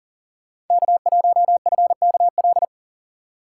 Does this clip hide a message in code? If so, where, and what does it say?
Yes: Morse code from 0.7 to 2.6 s, heard as a keyed tone, about 53 wpm, 700 Hz. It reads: K1FKL